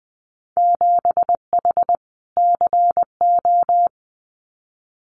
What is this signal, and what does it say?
Morse code, heard as a keyed tone, about 20 wpm, 700 Hz. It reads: 7H CO
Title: Winter Field Day Exchanges Encore 20wpm